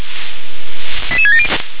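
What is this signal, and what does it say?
Five-tone selective call (ZVEI), heard as quick runs of five tones, tones of about 70 ms. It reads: 88670
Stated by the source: ZVEI five-tone call received on SDR.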